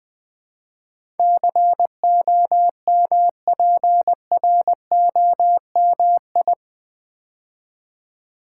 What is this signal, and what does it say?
Morse code, heard as a keyed tone, about 20 wpm, 700 Hz. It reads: COMPROMI
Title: Build A Word - 4 Letters At A Time - 10-Letter Words - 20wpm